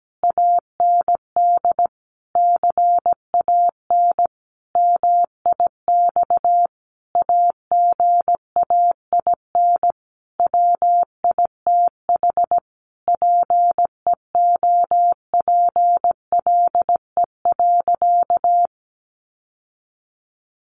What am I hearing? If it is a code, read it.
Morse code, heard as a keyed tone, about 17 wpm, 700 Hz. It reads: AND CAN MIX AGAIN WITH PEOPLE.
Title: Sentences from Top 1100 Words 17wpm